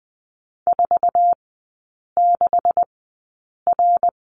Morse code, heard as a keyed tone, about 20 wpm, spaced as for 8 wpm, 700 Hz. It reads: 46R
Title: 127 Lesson 21e - Number 6 - Three Characters 20wpm